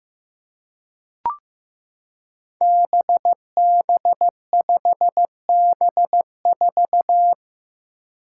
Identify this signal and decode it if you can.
Morse code, heard as a keyed tone, about 15 wpm, 700 Hz. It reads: BB5B4
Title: Five Challenging Letter-Number Combinations 15wpm